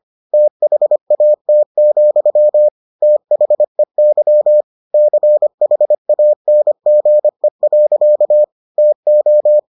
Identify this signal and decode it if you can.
Morse code, heard as a keyed tone, about 25 wpm, 600 Hz. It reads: THAT, THEY CHANGE. TO